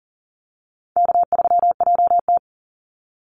Morse code, heard as a keyed tone, about 40 wpm, 700 Hz. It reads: K3JT
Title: Easy Call Signs - 40wpm